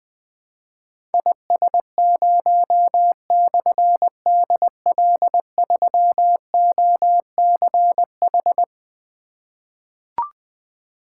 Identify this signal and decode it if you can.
Morse code, heard as a keyed tone, about 20 wpm, 700 Hz. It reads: IS0/DL3OCH
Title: Difficult Call Signs 20wpm